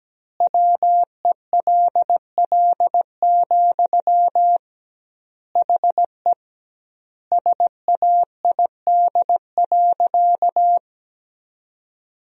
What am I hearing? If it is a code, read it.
Morse code, heard as a keyed tone, about 17 wpm, 700 Hz. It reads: WELL, HE SAID.
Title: 2x Word-Spacing - Sentences from Top 500 Words - 17wpm